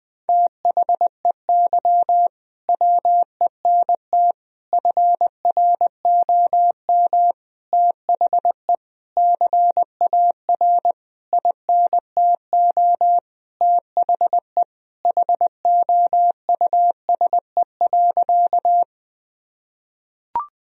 Morse code, heard as a keyed tone, about 20 wpm, 700 Hz. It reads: THEY WENT FROM THE CAR INTO THE HOUSE.
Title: Sentences from Top 1000 Words - Review Part 1 - 20wpm